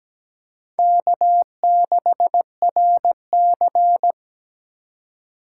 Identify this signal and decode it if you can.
Morse code, heard as a keyed tone, about 17 wpm, 700 Hz. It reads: K6RC
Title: K1USN SST QSO Practice - 17wpm